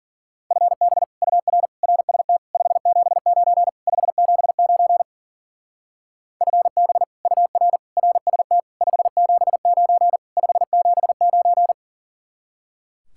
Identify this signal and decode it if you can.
Morse code, heard as a keyed tone, about 49 wpm, 700 Hz. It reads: FB UR RST 579 579 FB UR RST 579 579
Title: Speed-Racing Common QSO Phrases - 40wpm